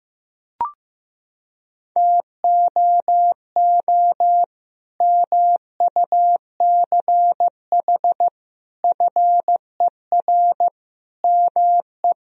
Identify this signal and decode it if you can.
Morse code, heard as a keyed tone, about 15 wpm, 700 Hz. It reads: TOO MUCH FER ME
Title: Sentences from Top 200 Words and Abbreviations 15wpm